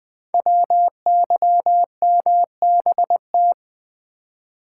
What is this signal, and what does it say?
Morse code, heard as a keyed tone, about 20 wpm, 700 Hz. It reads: WYMBT
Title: Code Groups - 5 Letters - 20wpm